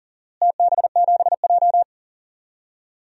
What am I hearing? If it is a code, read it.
Morse code, heard as a keyed tone, about 40 wpm, 700 Hz. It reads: TB7J
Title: DXCC-100 Call Signs - 4 Characters - Repeated 2x - 40wpm